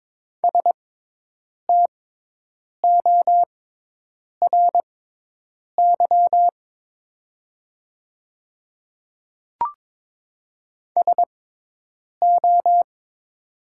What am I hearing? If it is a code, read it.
Morse code, heard as a keyed tone, about 22 wpm, spaced as for 8 wpm, 690 Hz. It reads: STORY SO
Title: 6x Character Spacing - Top 300 Words - 22wpm